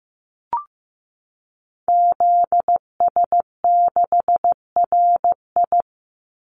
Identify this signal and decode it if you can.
Morse code, heard as a keyed tone, about 15 wpm, 700 Hz. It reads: ZS6RI
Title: Call Signs Repeated 3x - 15wpm